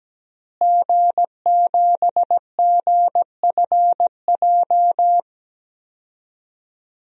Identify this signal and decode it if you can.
Morse code, heard as a keyed tone, about 17 wpm, 700 Hz. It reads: G7GFJ
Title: DXCC-100 Call Signs - 5 Characters - Repeated 3x - 17wpm